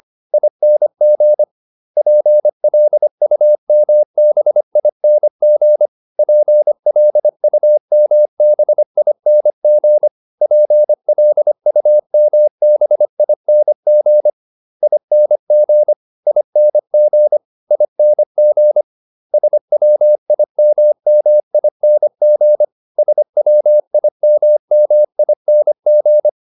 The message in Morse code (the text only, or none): ING PLUMBING PLUMBING PLUMBING ING ING ING SWIMMING SWIMMING